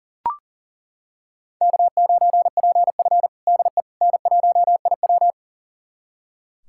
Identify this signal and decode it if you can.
Morse code, heard as a keyed tone, about 40 wpm, 700 Hz. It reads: K9PF DE N1IW
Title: USA Callsign DE Callsign - 4 Characters - 40wpm